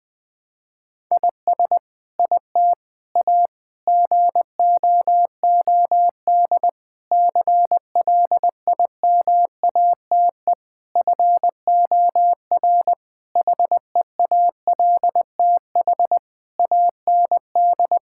Morse code, heard as a keyed tone, about 20 wpm, 700 Hz. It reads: IS IT A GOOD CLIMATE FOR HEALTH AND
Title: Sentences from Top 1600 Words 20wpm